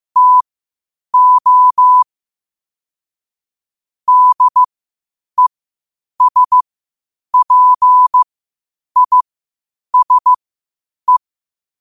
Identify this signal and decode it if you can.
Morse code, heard as a keyed tone, about 15 wpm, spaced as for 8 wpm, 1000 Hz. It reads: TO DESPISE
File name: qotdfiles / 8wpm